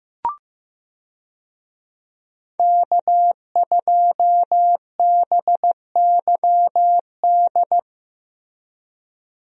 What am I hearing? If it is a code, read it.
Morse code, heard as a keyed tone, about 15 wpm, 700 Hz. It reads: K2BYD
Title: USA Call Signs - 5 Characters - Repeated 2x - 15wpm